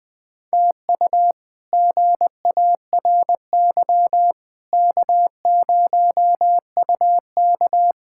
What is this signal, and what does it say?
Morse code, heard as a keyed tone, about 20 wpm, 700 Hz. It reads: TU GARY K0UK
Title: K1USN SST QSO Practice - 20wpm